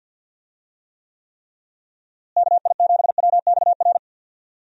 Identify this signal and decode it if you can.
Morse code, heard as a keyed tone, about 51 wpm, 700 Hz. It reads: KI7WXR
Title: USA Call Signs - 6 Characters - Repeated 3x - 50wpm